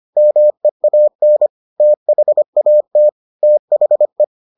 Morse code, heard as a keyed tone, about 25 wpm, 600 Hz. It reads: MEAN THAT THE